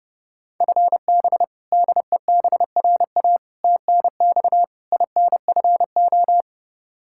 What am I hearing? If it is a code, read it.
Morse code, heard as a keyed tone, about 32 wpm, 700 Hz. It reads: FB DEBRA TNX INFO
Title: Practice QSO - Exchange 3 - More Info - 30wpm